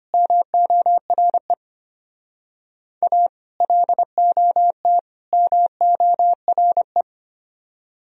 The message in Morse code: MORE A LOT MORE

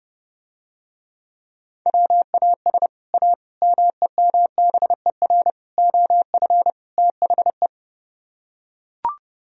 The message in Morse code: WAS A MEMBER OF THE

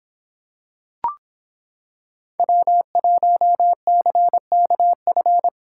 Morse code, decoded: W1CKF

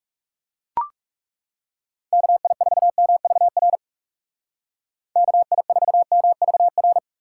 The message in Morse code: KI4MVR KI4MVR

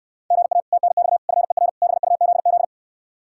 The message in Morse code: GA TNX FER BUZZ

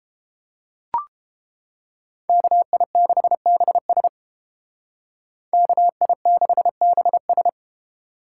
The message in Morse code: KI6BS KI6BS

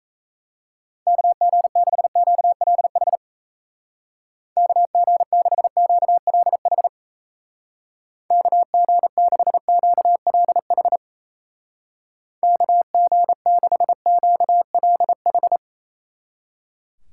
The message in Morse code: KG6QLH KG6QLH KG6QLH KG6QLH

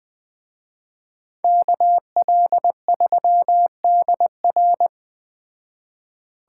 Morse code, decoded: KL3DR

